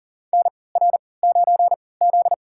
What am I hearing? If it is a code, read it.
Morse code, heard as a keyed tone, about 40 wpm, spaced as for 23 wpm, 700 Hz. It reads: NR9Z
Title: USA Call Signs - 4 Characters - 3x Character Spacing - 40wpm